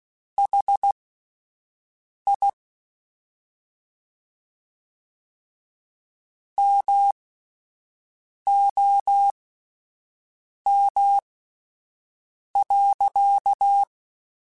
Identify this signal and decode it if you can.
Morse code, heard as a keyed tone, about 16 wpm, spaced as for 5 wpm, 800 Hz. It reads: HI MOM.
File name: (spaced) / QSOs Spaced-mp3s / WA7QOX 1 @10 sp